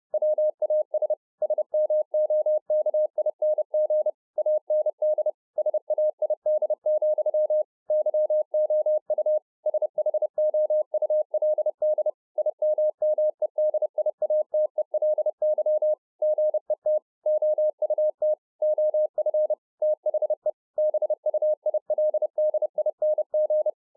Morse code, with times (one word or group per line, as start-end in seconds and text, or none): 0.1-1.1 WAS
1.4-4.1 SMOKING
4.4-5.3 AND
5.6-7.6 SAID,
7.9-9.4 YOU
9.7-12.1 SHOULD
12.4-15.9 IMMEDIATELY
16.2-17.0 GET
17.3-18.3 OUT
18.6-19.5 OF
19.8-20.5 THE
20.8-23.7 BUILDING